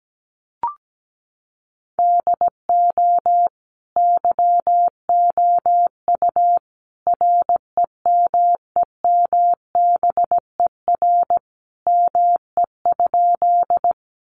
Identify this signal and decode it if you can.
Morse code, heard as a keyed tone, about 17 wpm, 700 Hz. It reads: DO YOU REMEMBER ME?
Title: Sentences from Top 400 Words 17wpm